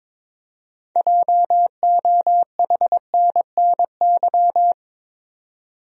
Morse code, heard as a keyed tone, about 22 wpm, 700 Hz. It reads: JOHNNY
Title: USA Common Names - 22wpm